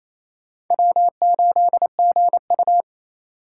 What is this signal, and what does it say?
Morse code, heard as a keyed tone, about 28 wpm, 700 Hz. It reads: W8GU
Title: K1USN SST QSO Practice - 28wpm